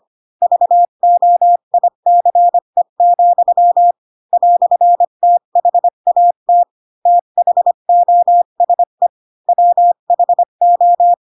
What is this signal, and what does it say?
Morse code, heard as a keyed tone, about 25 wpm, 700 Hz. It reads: VOICE, "THAT THOSE WHO
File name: Pinocchio / Pinocchio ch4 20 25 ik1hge